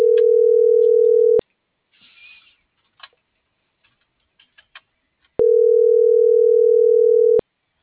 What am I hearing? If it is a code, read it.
Ringback tone (North America), 440 + 480 Hz.